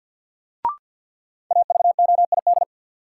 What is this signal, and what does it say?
Morse code, heard as a keyed tone, about 47 wpm, 700 Hz. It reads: AVOID
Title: Sentences from Top 1000 Words - Review Part 6 - New Word Drill - 50wpm